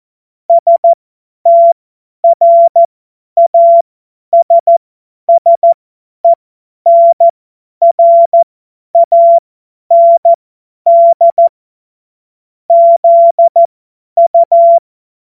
Morse code, heard as a keyed tone, about 14 wpm, spaced as for 10 wpm, 680 Hz. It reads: STRASSENRAND ZU
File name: Morsekurs / Witze / Witz 24.Oktober 2023 50 F70